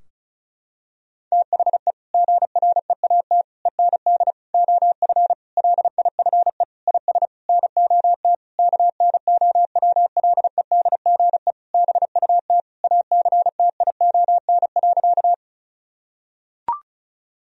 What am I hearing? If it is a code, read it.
Morse code, heard as a keyed tone, about 36 wpm, 700 Hz. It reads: THE GREAT END OF LIFE IS NOT KNOWLEDGE BUT ACTION.
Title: Sentences from Top 1000 Words - Review Part 4 - 35wpm